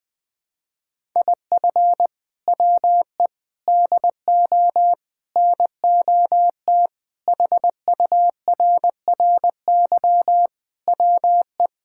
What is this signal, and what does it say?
Morse code, heard as a keyed tone, about 21 wpm, 700 Hz. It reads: IF WE DO NOT HURRY WE
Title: Sentences with 4-Word Phrases - 20wpm